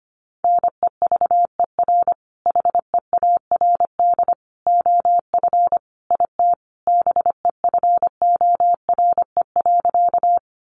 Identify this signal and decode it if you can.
Morse code, heard as a keyed tone, about 25 wpm, 700 Hz. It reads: NEVER HEARD OF IT BEFORE.
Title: Sentences from Top 400 Words 25wpm